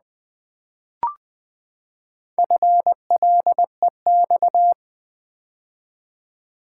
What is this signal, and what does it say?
Morse code, heard as a keyed tone, about 20 wpm, 700 Hz. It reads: FLEX
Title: Build A Word - 4 Letters At A Time - 11-Letter Words - 20wpm